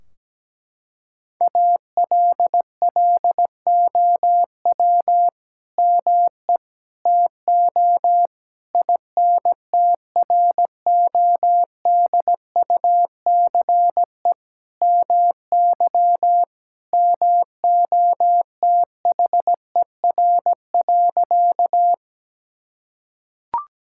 Morse code, heard as a keyed tone, about 17 wpm, 700 Hz. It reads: ALLOW ME TO INTRODUCE MY MOTHER.
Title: Sentences from Top 1600 Words 17wpm